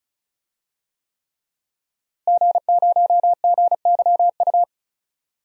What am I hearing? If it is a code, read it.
Morse code, heard as a keyed tone, about 34 wpm, 700 Hz. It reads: G0GYU
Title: DXCC-100 Call Signs - 5 Characters - Repeated 3x - 35wpm